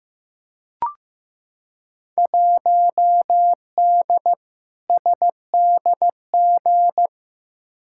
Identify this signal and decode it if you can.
Morse code, heard as a keyed tone, about 15 wpm, 700 Hz. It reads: 1D SDG
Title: Field Day Exchanges - 15wpm